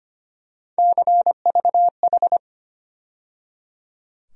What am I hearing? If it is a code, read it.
Morse code, heard as a keyed tone, about 25 wpm, 700 Hz. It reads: CVH
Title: USA Call Sign Postfixes - 3 Letters 25wpm